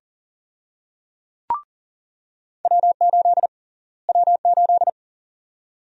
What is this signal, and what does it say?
Morse code, heard as a keyed tone, about 40 wpm, 700 Hz. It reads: W8 W8
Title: USA Call Sign Prefixes - Repeated 2x - 40wpm